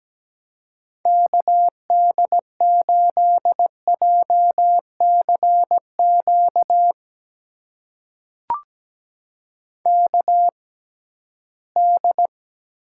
Morse code, heard as a keyed tone, about 17 wpm, spaced as for 13 wpm, 700 Hz. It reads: KD8JCQ KD
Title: USA Call Signs - 6 Characters - 6x Character Spacing - 17wpm